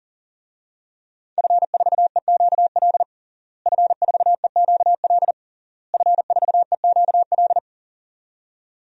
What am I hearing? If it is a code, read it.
Morse code, heard as a keyed tone, about 40 wpm, 700 Hz. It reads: F4EQL F4EQL F4EQL
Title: DXCC-100 Call Signs - 5 Characters - Repeated 3x - 40wpm